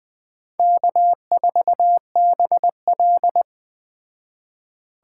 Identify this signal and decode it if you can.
Morse code, heard as a keyed tone, about 21 wpm, 700 Hz. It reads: K4BL
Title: Speed Racing - USA 1x2 Call Signs 17wpm